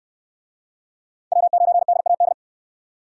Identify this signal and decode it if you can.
Morse code, heard as a keyed tone, about 62 wpm, 700 Hz. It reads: W9DAD